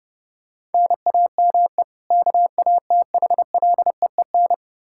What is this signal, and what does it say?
Morse code, heard as a keyed tone, about 30 wpm, 700 Hz. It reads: NAME KATHLEEN
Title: Practice QSO - Exchange 1 - Beginning - 30wpm